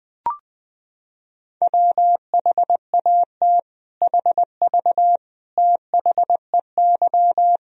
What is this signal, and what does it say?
Morse code, heard as a keyed tone, about 21 wpm, 700 Hz. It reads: WHAT HV THEY
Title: Sentences from Top 500 Words and Abbreviations 20wpm